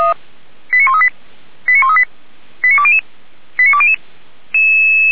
Five-tone selective call (ZVEI), heard as quick runs of five tones, tones of about 70 ms. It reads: 89138 89138 89200 89200
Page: SDR/ZVEI